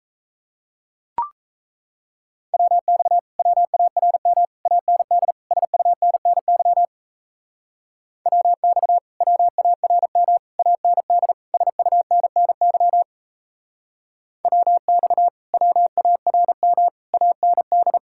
Morse code, heard as a keyed tone, about 36 wpm, 700 Hz. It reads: WX WARM AND SUNNY WX WARM AND SUNNY WX WARM AND